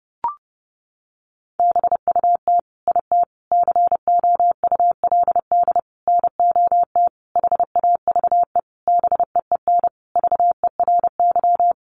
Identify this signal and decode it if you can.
Morse code, heard as a keyed tone, about 30 wpm, 700 Hz. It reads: BUT IT COULD NOT HAVE BEEN VERY
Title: Sentences from Top 1400 Words 30wpm